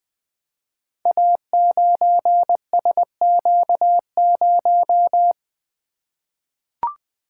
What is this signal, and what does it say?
Morse code, heard as a keyed tone, about 20 wpm, 700 Hz. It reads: A9SQ0
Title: Code Groups - 5 Letter:Numbers - 20wpm